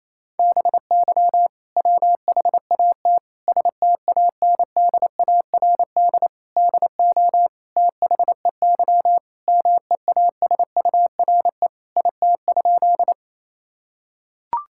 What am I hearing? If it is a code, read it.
Morse code, heard as a keyed tone, about 28 wpm, 700 Hz. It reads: BY WHAT STANDARD DO THEY MEASURE IT?
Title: Sentences from Top 1000 Words - Review Part 6 - 28wpm